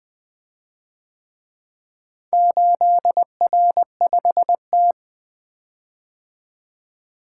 Morse code, heard as a keyed tone, about 24 wpm, 700 Hz. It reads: 8R5T
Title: Speed-Racing - Four Letter-Number Combinations - 20wpm